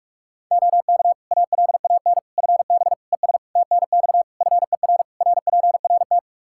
Morse code, heard as a keyed tone, about 46 wpm, 700 Hz. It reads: OK ALAN FB ES TNX FER RPRT
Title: Practice QSO - Exchange 2 - Info - 45wpm